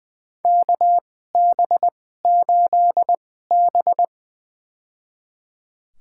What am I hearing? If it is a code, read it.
Morse code, heard as a keyed tone, about 20 wpm, spaced as for 14 wpm, 700 Hz. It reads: KB8B
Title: USA Call Signs - 4 Characters - 2x Character Spacing - 20wpm